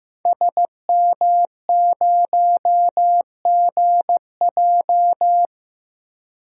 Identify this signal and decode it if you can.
Morse code, heard as a keyed tone, about 15 wpm, 700 Hz. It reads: SM0GJ
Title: DXCC-100 Call Signs - 5 Characters - Repeated 3x - 15wpm